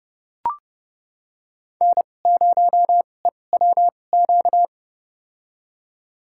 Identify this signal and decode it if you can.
Morse code, heard as a keyed tone, about 31 wpm, spaced as for 22 wpm, 700 Hz. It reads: N0EWQ
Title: USA Call Signs - 5 Characters - 2x Character Spacing - 30wpm